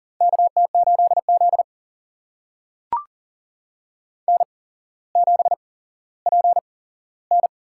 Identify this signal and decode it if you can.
Morse code, heard as a keyed tone, about 40 wpm, spaced as for 17 wpm, 700 Hz. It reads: KT8Z N7PN
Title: USA Call Signs - 4 Characters - 8x Character Spacing - 40wpm